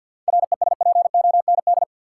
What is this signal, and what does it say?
Morse code, heard as a keyed tone, about 52 wpm, 700 Hz. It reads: RESPOND
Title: Seven Letter Words - 50wpm